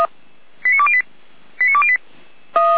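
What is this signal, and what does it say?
Five-tone selective call (ZVEI), heard as quick runs of five tones, tones of about 80 ms. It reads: 89208 89208
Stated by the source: ZVEI five-tone selective call received via SDR.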